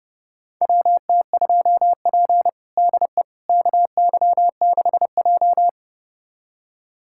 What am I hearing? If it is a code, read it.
Morse code, heard as a keyed tone, about 30 wpm, 700 Hz. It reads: WT2P DE KY6J
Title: USA Callsign DE Callsign - 4 Characters - 30wpm